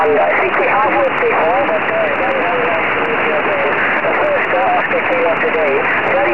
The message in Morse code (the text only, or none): none